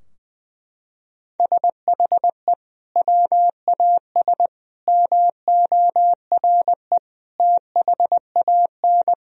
Morse code, heard as a keyed tone, about 21 wpm, 700 Hz. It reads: SHE WAS MORE THAN
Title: Sentences from Top 1000 Words - Review Part 3 - 20wpm